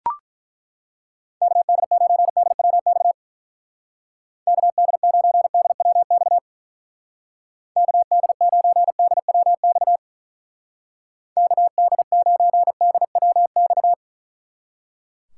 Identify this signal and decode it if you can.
Morse code, heard as a keyed tone, about 42 wpm, 700 Hz. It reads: KD9DWX KD9DWX KD9DWX KD9DWX
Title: Speed Racing - USA 2x3 Call Signs 35wpm